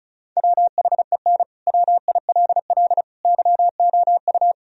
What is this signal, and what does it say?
Morse code, heard as a keyed tone, about 34 wpm, 700 Hz. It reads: WHEN WILL YOU
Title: Sentences from Top 1000 Words - Review Part 4 - 35wpm